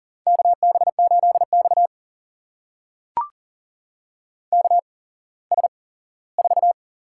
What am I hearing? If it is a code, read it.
Morse code, heard as a keyed tone, about 40 wpm, spaced as for 19 wpm, 700 Hz. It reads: KB8X KS4
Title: USA Call Signs - 8x Character Spacing - 40wpm